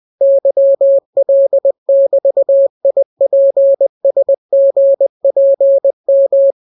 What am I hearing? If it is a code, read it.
Morse code, heard as a keyed tone, about 20 wpm, 550 Hz. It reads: YL<BT>IPSGPM